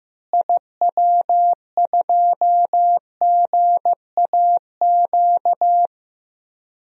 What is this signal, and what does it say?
Morse code, heard as a keyed tone, about 15 wpm, 700 Hz. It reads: IW2GAQ
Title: DXCC-100 Call Signs - 6 Characters - Repeated 3x - 15wpm